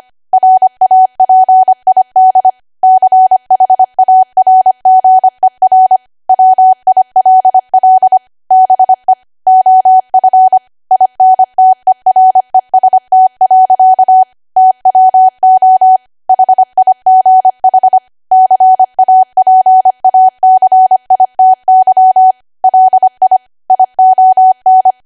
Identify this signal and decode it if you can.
Morse code, heard as a keyed tone, about 25 wpm, 750 Hz. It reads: RAPID CHARGER WILL BE OF INTEREST. TWO HIGH CAPACITY LI ION
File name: Archive / 25 WPM / 25WPM